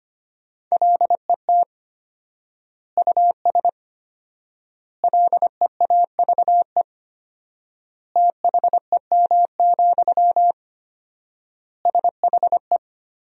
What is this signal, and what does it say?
Morse code, heard as a keyed tone, about 25 wpm, 700 Hz. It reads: LET US LEAVE THEM, SHE
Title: 4x Word-Spacing - Sentences from Top 400 Words - 25wpm